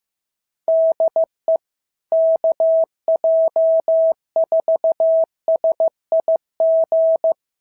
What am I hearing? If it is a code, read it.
Morse code, heard as a keyed tone, about 15 wpm, 650 Hz. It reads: DE KJ4SIG
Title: SOTA QSO Practice 15wpm